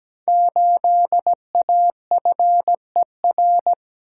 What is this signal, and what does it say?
Morse code, heard as a keyed tone, about 17 wpm, 700 Hz. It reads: 8AFER
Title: Code Groups - 5 Letter:Numbers - 17wpm